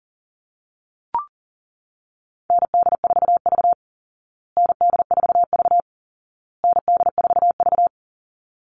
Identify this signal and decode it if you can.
Morse code, heard as a keyed tone, about 40 wpm, 700 Hz. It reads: ND4V ND4V ND4V
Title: Call Signs Repeated 2x - 40wpm